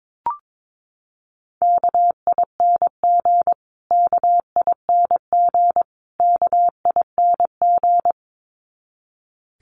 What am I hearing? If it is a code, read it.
Morse code, heard as a keyed tone, about 22 wpm, 700 Hz. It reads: KING KING KING
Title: Top 500 Words Repeated 22wpm